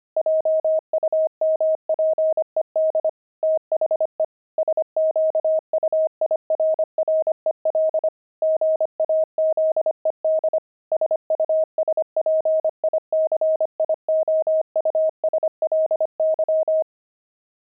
Morse code, heard as a keyed tone, about 25 wpm, 630 Hz. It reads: JUMPED THE SQUIRREL GAZED SUSPICIOUSLY
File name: pangram-fs-25-cs-25-squirrel